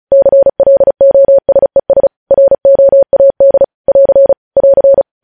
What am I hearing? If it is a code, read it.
Morse code, heard as a keyed tone, about 36 wpm, 550 Hz. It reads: CLOSES ROAD <AR> <AR>
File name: podcast / News headlines-1x-35x35